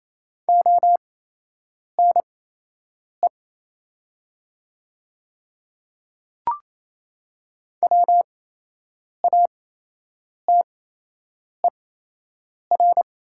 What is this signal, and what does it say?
Morse code, heard as a keyed tone, about 28 wpm, spaced as for 8 wpm, 700 Hz. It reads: ONE WATER
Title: 8x Character Spacing - Top 100 Words - 28wpm